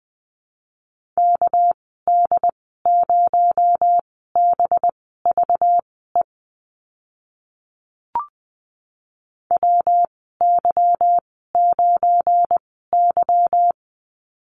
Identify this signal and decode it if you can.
Morse code, heard as a keyed tone, about 19 wpm, spaced as for 14 wpm, 700 Hz. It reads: KD0BVE WY9Y
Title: 238 Lesson 38g - Letter q - Call Signs 20wpm